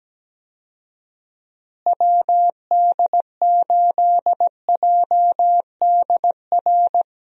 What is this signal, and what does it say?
Morse code, heard as a keyed tone, about 17 wpm, 700 Hz. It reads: WD8JDR
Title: USA Call Signs - 6 Characters - Repeated 3x - 17wpm